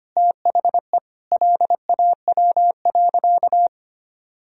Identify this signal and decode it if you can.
Morse code, heard as a keyed tone, about 25 wpm, 700 Hz. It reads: THE LAW.
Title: Sentences from Top 1000 Words - Review Part 3 - 25wpm